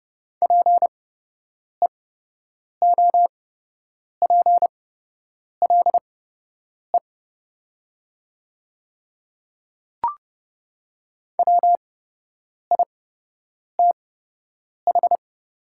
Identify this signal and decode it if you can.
Morse code, heard as a keyed tone, about 30 wpm, spaced as for 8 wpm, 700 Hz. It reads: PEOPLE WITH
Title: 8x Character Spacing - Top 100 Words - 30wpm